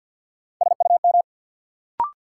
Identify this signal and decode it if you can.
Morse code, heard as a keyed tone, about 49 wpm, 700 Hz. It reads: SUM
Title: Sentences from Top 1000 Words - Review Part 6 - New Word Drill - 50wpm